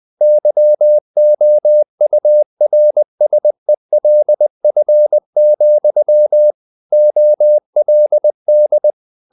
Morse code, heard as a keyed tone, about 20 wpm, 600 Hz. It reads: YOURSELF, OLD
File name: the old man and the sea part 2 20wpm